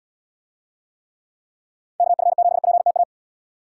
Morse code, heard as a keyed tone, about 78 wpm, 700 Hz. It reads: KF7PTA